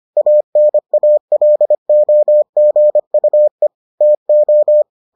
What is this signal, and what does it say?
Morse code, heard as a keyed tone, about 25 wpm, 600 Hz. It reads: ANALOGUE TO